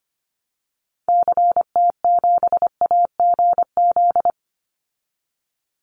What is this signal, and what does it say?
Morse code, heard as a keyed tone, about 25 wpm, 700 Hz. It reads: CT7AGZ
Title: Call Signs Repeated 2x - 25wpm